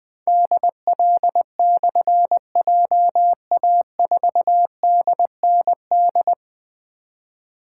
Morse code, heard as a keyed tone, about 20 wpm, 700 Hz. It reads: DL/JA4DND
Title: Difficult Call Signs 20wpm